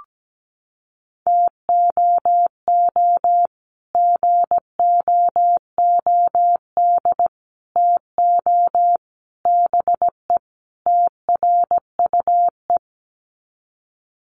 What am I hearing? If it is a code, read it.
Morse code, heard as a keyed tone, about 17 wpm, 700 Hz. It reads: TOO GOOD TO BE TRUE
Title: Sets of 5 Words Encore 17wpm